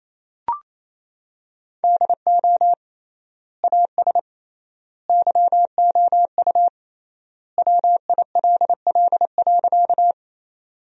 Morse code, heard as a keyed tone, about 28 wpm, 700 Hz. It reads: DO AS YOU WILL.
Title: 3x Word-Spacing - Sentences from Top 100 Words - 28wpm